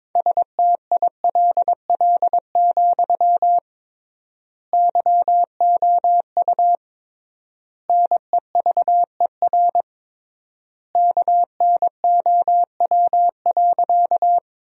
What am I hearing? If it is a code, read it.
Morse code, heard as a keyed tone, about 22 wpm, 700 Hz. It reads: STILL, YOU NEVER KNOW.
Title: 3x Word-Spacing - Sentences from Top 300 Words - 22wpm